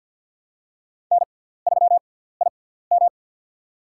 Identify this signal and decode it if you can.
Morse code, heard as a keyed tone, about 47 wpm, spaced as for 17 wpm, 700 Hz. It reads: N3IM
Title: USA Call Signs - 4 Characters - 6x Character Spacing - 50wpm